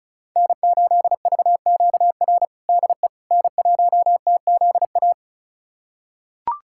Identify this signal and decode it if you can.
Morse code, heard as a keyed tone, about 36 wpm, 700 Hz. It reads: N8VQR DE N1TZA
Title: USA Callsign DE Callsign - 5 Characters - 35wpm